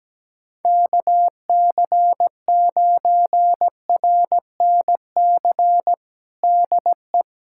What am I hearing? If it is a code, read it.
Morse code, heard as a keyed tone, about 17 wpm, 700 Hz. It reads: KC9RNC DE
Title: USA Callsign DE Callsign 17wpm